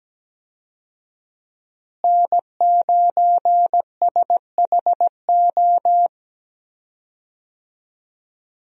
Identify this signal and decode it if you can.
Morse code, heard as a keyed tone, about 17 wpm, 700 Hz. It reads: N9SHO
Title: USA Call Signs - 5 Characters - Repeated 2x - 17wpm